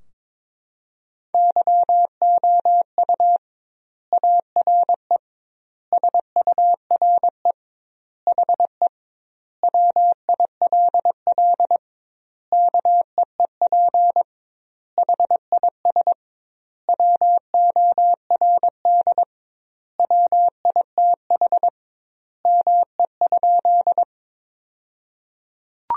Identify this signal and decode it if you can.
Morse code, heard as a keyed tone, about 22 wpm, 700 Hz. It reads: YOU ARE SURE HE WILL KEEP HIS WORD WITH ME?
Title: 2x Word-Spacing - Sentences from Top 300 Words - 22wpm